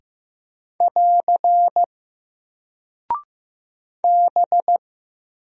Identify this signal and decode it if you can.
Morse code, heard as a keyed tone, about 15 wpm, 700 Hz. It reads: <AR> B
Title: Single Character (Letter:Number:Punctuation:Prosign) 15wpm